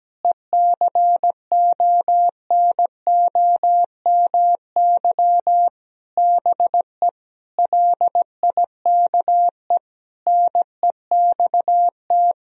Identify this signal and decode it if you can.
Morse code, heard as a keyed tone, about 17 wpm, 700 Hz. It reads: ECONOMY BE LIKE NEXT
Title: Sentences from Top 1000 Words - Review Part 3 - 17wpm